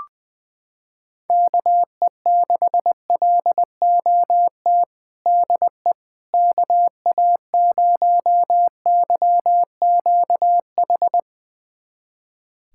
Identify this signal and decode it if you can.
Morse code, heard as a keyed tone, about 20 wpm, 700 Hz. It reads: KE6LOT DE KA0YQH
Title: USA Callsign DE Callsign 20wpm